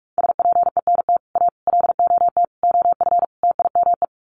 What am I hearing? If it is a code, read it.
Morse code, heard as a keyed tone, about 44 wpm, 700 Hz. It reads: SPENT A LOT OF TIME
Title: Sets of 5 Words 45wpm